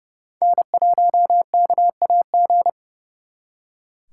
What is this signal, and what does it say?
Morse code, heard as a keyed tone, about 38 wpm, 700 Hz. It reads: N1KAG